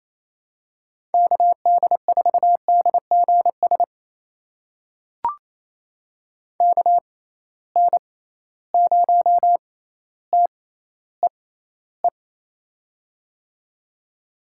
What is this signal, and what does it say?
Morse code, heard as a keyed tone, about 28 wpm, spaced as for 14 wpm, 700 Hz. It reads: KD4DGS KN0TEE
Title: USA Call Signs - 6 Characters - 6x Character Spacing - 28wpm